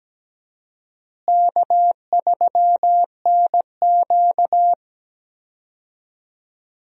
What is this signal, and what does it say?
Morse code, heard as a keyed tone, about 17 wpm, 700 Hz. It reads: K3NQ
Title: USA Call Signs - 4 Characters - Repeated 2x - 17wpm